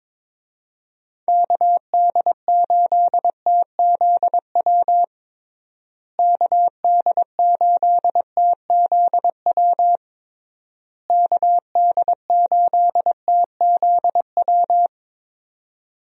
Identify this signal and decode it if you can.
Morse code, heard as a keyed tone, about 23 wpm, 700 Hz. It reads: KD8TZW KD8TZW KD8TZW
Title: USA Call Signs - 6 Characters - Repeated 3x - 22wpm